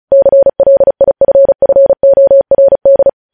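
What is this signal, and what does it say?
Morse code, heard as a keyed tone, about 36 wpm, 550 Hz. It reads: CLIFFORD
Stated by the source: News headlines sent in Morse code at 35 WPM.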